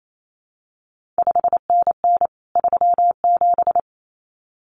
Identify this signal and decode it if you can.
Morse code, heard as a keyed tone, about 28 wpm, 700 Hz. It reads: 5NN 37
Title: CQ WW DX Contest Exchange Encore - 28wpm